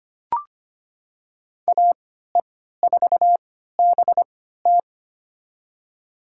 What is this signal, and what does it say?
Morse code, heard as a keyed tone, about 25 wpm, spaced as for 14 wpm, 700 Hz. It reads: AE4BT
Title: USA Call Signs - 5 Characters - 3x Character Spacing - 25wpm